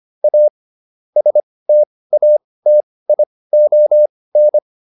Morse code, heard as a keyed tone, about 25 wpm, spaced as for 18 wpm, 600 Hz. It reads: A STATION